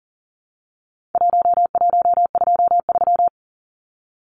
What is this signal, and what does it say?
Morse code, heard as a keyed tone, about 40 wpm, 700 Hz. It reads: 1123
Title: Numbers - 4-Digit 40wpm